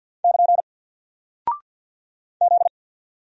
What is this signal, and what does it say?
Morse code, heard as a keyed tone, about 48 wpm, 700 Hz. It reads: <KN> Z
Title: Single Character (Letter:Number:Punctuation:Prosign) Alt 50wpm